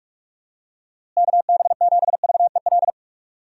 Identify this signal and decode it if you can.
Morse code, heard as a keyed tone, about 46 wpm, 700 Hz. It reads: KB7VEL
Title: USA Call Signs - 6 Characters - Repeated 3x - 45wpm